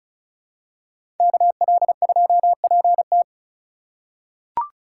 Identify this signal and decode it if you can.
Morse code, heard as a keyed tone, about 36 wpm, 700 Hz. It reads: KL2PT
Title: USA Call Signs - 5 Characters - 4x Character Spacing - 35wpm